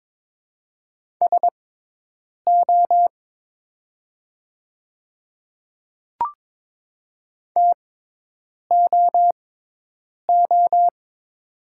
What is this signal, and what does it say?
Morse code, heard as a keyed tone, about 22 wpm, spaced as for 8 wpm, 700 Hz. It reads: SO TOO
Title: 6x Character Spacing - Top 200 Words - 22wpm